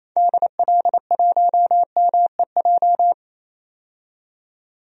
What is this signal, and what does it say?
Morse code, heard as a keyed tone, about 28 wpm, 700 Hz. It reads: DL1MEJ
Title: DXCC-100 Call Signs - 6 Characters - 28wpm